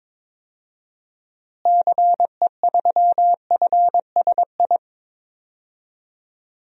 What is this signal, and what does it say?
Morse code, heard as a keyed tone, about 22 wpm, 700 Hz. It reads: CE3FSI
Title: DXCC-100 Call Signs - 6 Characters - Repeated 2x - 22wpm